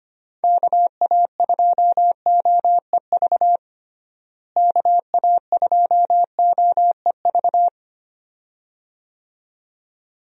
Morse code, heard as a keyed tone, about 25 wpm, 700 Hz. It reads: KA2OEV KA2OEV
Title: USA Call Signs - 6 Characters - Repeated 2x - 25wpm